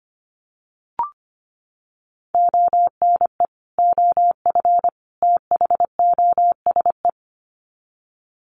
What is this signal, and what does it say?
Morse code, heard as a keyed tone, about 25 wpm, 700 Hz. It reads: ONE OF THOSE
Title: Sets of 3 Words 25wpm